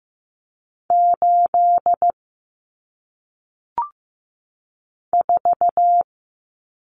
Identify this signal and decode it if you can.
Morse code, heard as a keyed tone, about 15 wpm, 700 Hz. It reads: 8 4
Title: Numbers - 1-Digit 15wpm